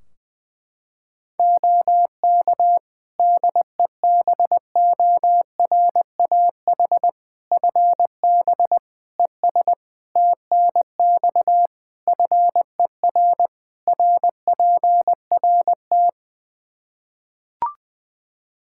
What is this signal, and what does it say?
Morse code, heard as a keyed tone, about 20 wpm, 700 Hz. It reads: OK DEBORAH FB ES TNX FER RPRT
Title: Practice QSO - Exchange 2 - Info - 20wpm